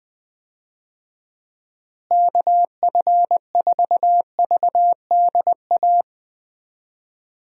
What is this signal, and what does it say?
Morse code, heard as a keyed tone, about 19 wpm, 700 Hz. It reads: KF4VDA
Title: USA Call Signs - 6 Characters - Repeated 2x - 20wpm